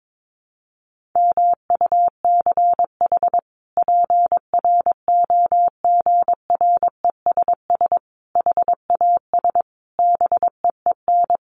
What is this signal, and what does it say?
Morse code, heard as a keyed tone, about 22 wpm, 700 Hz. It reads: MUCH PROGRESS HAS BEEN
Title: Sentences from Top 1400 Words 22wpm